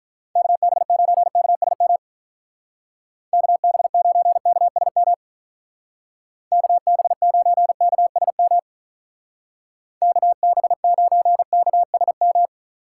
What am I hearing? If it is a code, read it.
Morse code, heard as a keyed tone, about 45 wpm, 700 Hz. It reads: KB9KSM KB9KSM KB9KSM KB9KSM